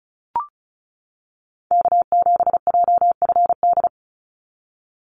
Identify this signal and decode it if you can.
Morse code, heard as a keyed tone, about 34 wpm, 700 Hz. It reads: K7JFD
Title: NA SOTA Activator Callsigns 35wpm